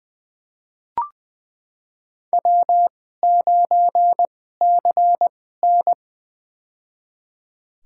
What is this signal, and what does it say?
Morse code, heard as a keyed tone, about 20 wpm, spaced as for 14 wpm, 700 Hz. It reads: W9CN
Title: USA Call Signs - 4 Characters - 2x Character Spacing - 20wpm